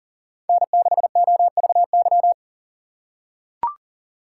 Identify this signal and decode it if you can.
Morse code, heard as a keyed tone, about 40 wpm, 700 Hz. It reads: N6OVY
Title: USA Call Signs - 5 Characters - 4x Character Spacing - 40wpm